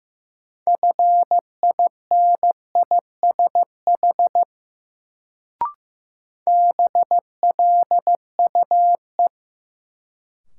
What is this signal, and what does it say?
Morse code, heard as a keyed tone, about 15 wpm, 700 Hz. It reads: FINISH BLUE
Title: Sentences from Top 1000 Words - Review Part 3 - New Word Drill - 15wpm